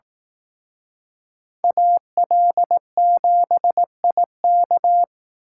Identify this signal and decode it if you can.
Morse code, heard as a keyed tone, about 18 wpm, 700 Hz. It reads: AL7IK
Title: Speed Racing - USA 2x2 Call Signs 15wpm